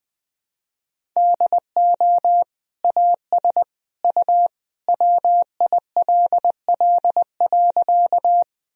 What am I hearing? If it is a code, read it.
Morse code, heard as a keyed tone, about 21 wpm, 700 Hz. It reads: DO AS U WILL.
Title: Sentences from Top 100 Words and Abbreviations 20wpm